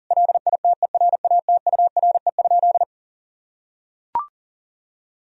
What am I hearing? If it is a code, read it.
Morse code, heard as a keyed tone, about 40 wpm, 700 Hz. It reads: LITERATURE?
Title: Sentences from Top 1000 Words - Review Part 3 - 40wpm